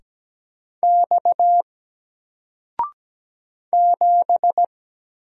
Morse code, heard as a keyed tone, about 17 wpm, 700 Hz. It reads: X 7
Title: Single Letter-Number 17wpm